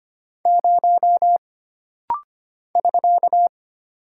Morse code, heard as a keyed tone, about 25 wpm, 700 Hz. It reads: 0 <SK>
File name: Single Character (Letter:Number:Punctuation:Prosign) - Rapid-Fire - 25wpm